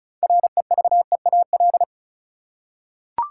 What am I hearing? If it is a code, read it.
Morse code, heard as a keyed tone, about 36 wpm, 700 Hz. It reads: REVEAL
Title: Six Letter Words - 35wpm